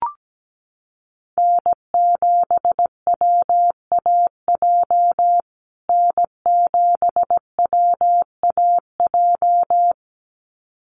Morse code, heard as a keyed tone, about 17 wpm, 700 Hz. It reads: N7WAJ N7WAJ
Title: USA Call Signs Repeated 2x 17wpm